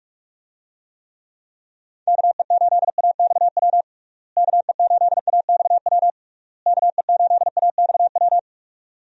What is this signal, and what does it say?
Morse code, heard as a keyed tone, about 41 wpm, 700 Hz. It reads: KE8AXW KE8AXW KE8AXW
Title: USA Call Signs - 6 Characters - Repeated 3x - 45wpm